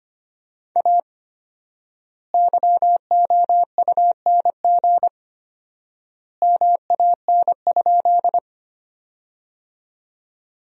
Morse code, heard as a keyed tone, about 25 wpm, 700 Hz. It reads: A YOUNG MAN?
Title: 4x Word-Spacing - Sentences from Top 400 Words - 25wpm